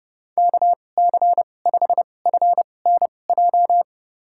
Morse code, heard as a keyed tone, about 30 wpm, spaced as for 22 wpm, 700 Hz. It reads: KC5FNJ
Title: USA Call Signs - 2x Character Spacing - 30wpm